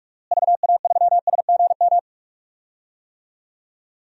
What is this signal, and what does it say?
Morse code, heard as a keyed tone, about 41 wpm, 700 Hz. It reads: VA3SGM
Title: DXCC-100 Call Signs - 6 Characters - Repeated 2x - 45wpm